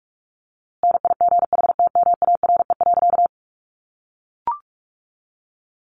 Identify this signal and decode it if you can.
Morse code, heard as a keyed tone, about 44 wpm, 700 Hz. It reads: NIGHTMARE.
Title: A Sound Of Thunder 45wpm